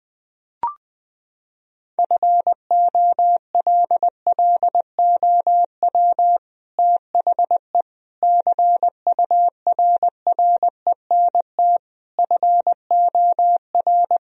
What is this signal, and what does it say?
Morse code, heard as a keyed tone, about 21 wpm, 700 Hz. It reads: FOLLOW THE CURRENT FOR
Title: Sentences from Top 1000 Words - Review Part 4 - 20wpm